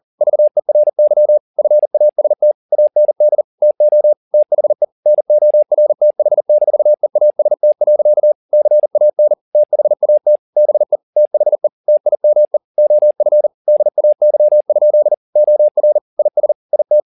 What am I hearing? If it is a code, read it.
Morse code, heard as a keyed tone, about 40 wpm, 600 Hz. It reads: VERY FAST AND TO THE NORTH-EAST. CAN THAT BE THE TIME OF DAY? OR IS IT